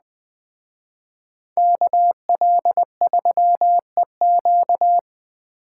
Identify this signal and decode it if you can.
Morse code, heard as a keyed tone, about 20 wpm, 700 Hz. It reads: KL3EQ
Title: Speed Racing - USA 2x2 Call Signs 17wpm